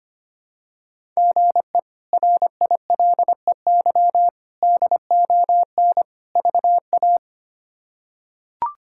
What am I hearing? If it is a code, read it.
Morse code, heard as a keyed tone, about 25 wpm, 700 Hz. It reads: GE RILEY DON VA
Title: K1USN SST QSO Practice - 25wpm